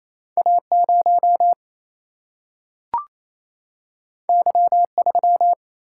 Morse code, heard as a keyed tone, about 28 wpm, 700 Hz. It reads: A 0 Y 3
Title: Single-Letter Single-Number 28wpm